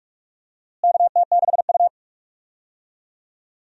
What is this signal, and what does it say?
Morse code, heard as a keyed tone, about 45 wpm, 700 Hz. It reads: KT6U
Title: USA Call Signs - 4 Characters - Repeated 2x - 45wpm